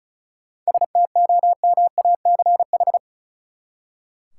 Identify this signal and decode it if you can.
Morse code, heard as a keyed tone, about 34 wpm, 700 Hz. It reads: STOMACH
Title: Seven Letter Words - 35wpm